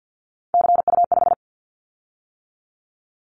Morse code, heard as a keyed tone, about 50 wpm, 700 Hz. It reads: CU5
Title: Callsign Prefixes for Top 50 DXCC - 50wpm